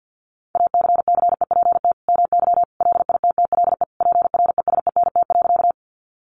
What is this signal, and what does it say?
Morse code, heard as a keyed tone, about 50 wpm, 700 Hz. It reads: ACCEPT MY LITTLE PRESENT.
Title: Sentences from Top 1300 Words 50wpm